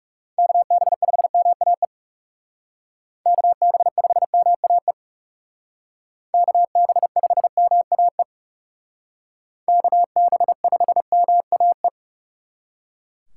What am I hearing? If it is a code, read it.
Morse code, heard as a keyed tone, about 39 wpm, 700 Hz. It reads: KB5MAE KB5MAE KB5MAE KB5MAE